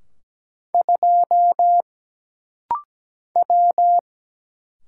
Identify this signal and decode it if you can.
Morse code, heard as a keyed tone, about 17 wpm, 700 Hz. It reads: 2 W
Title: Single Letter-Number - Rapid-Fire 17wpm